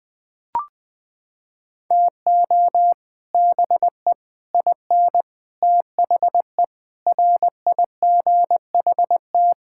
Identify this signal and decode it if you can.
Morse code, heard as a keyed tone, about 21 wpm, 700 Hz. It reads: TO BE IN THE RIGHT
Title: Sets of 9 Words - 20wpm